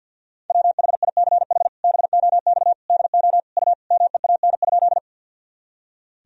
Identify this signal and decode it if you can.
Morse code, heard as a keyed tone, about 50 wpm, 700 Hz. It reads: WHICH BOX DO U MEAN?
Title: Sentences from Top 500 Words and Abbreviations 50wpm